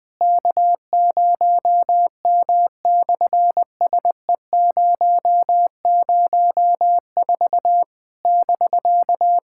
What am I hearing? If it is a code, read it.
Morse code, heard as a keyed tone, about 18 wpm, 700 Hz. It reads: K0M/SE004 <BK>
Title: SOTA QSO Practice 20wpm